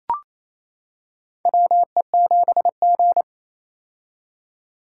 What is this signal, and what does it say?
Morse code, heard as a keyed tone, about 28 wpm, 700 Hz. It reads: WE7G
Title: K1USN SST QSO Practice - 28wpm